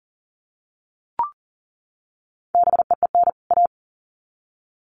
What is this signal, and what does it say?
Morse code, heard as a keyed tone, about 40 wpm, 700 Hz. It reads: BEEN A
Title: Sets of 2 Words 40wpm